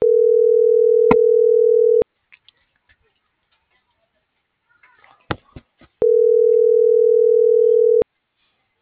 Ringback tone (North America), 440 + 480 Hz.